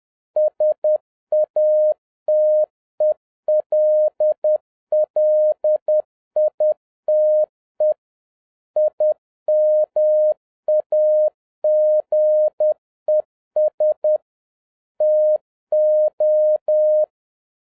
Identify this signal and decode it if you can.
Morse code, heard as a keyed tone, about 10 wpm, 610 Hz. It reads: SATELLITE IMAGES TO